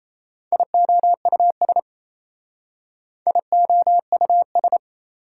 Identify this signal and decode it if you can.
Morse code, heard as a keyed tone, about 32 wpm, 700 Hz. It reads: IOUS IOUS
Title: Speed-Racing Prefixes and Suffixes 28wpm